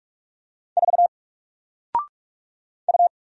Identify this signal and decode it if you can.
Morse code, heard as a keyed tone, about 44 wpm, 700 Hz. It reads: 4 U
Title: Single Character (Letter:Number:Punctuation:Prosign) 45wpm